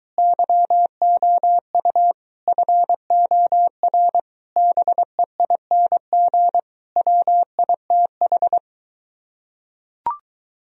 Morse code, heard as a keyed tone, about 22 wpm, 700 Hz. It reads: YOU FOR BEING WITH